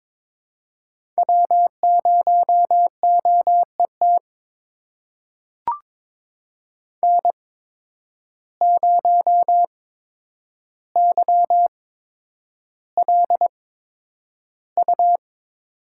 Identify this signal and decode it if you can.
Morse code, heard as a keyed tone, about 22 wpm, spaced as for 9 wpm, 700 Hz. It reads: W0OET N0YLU
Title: USA Call Signs - 5 Characters - 8x Character Spacing - 22wpm